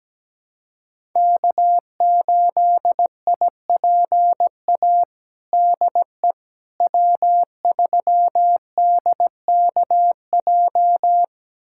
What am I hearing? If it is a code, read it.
Morse code, heard as a keyed tone, about 17 wpm, 700 Hz. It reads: K8IPA DE W3DKJ
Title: USA Callsign DE Callsign - 5 Characters - 17wpm